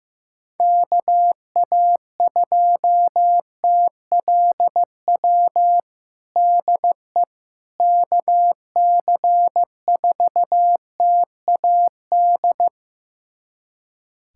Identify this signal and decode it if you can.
Morse code, heard as a keyed tone, about 15 wpm, 700 Hz. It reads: KA2TLW DE KC4TAD
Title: USA Callsign DE Callsign - 6 Characters - 15wpm